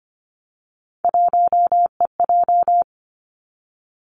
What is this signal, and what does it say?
Morse code, heard as a keyed tone, about 25 wpm, 700 Hz. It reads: 1EJ
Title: Three Challenging Letter-Number Combinations 25wpm